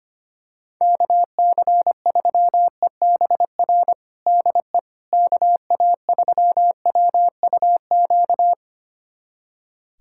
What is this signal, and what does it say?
Morse code, heard as a keyed tone, about 25 wpm, 700 Hz. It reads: KC3EBR DE KA3WUQ
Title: USA Callsign DE Callsign - 6 Characters - 25wpm